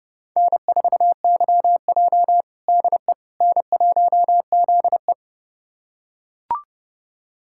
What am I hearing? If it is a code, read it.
Morse code, heard as a keyed tone, about 30 wpm, 700 Hz. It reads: N4YJ DE N1ZE
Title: USA Callsign DE Callsign - 4 Characters - 30wpm